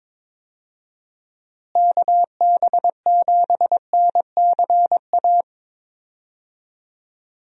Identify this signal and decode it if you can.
Morse code, heard as a keyed tone, about 22 wpm, 700 Hz. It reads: KB7NCA
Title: USA Call Signs - 6 Characters - Repeated 2x - 22wpm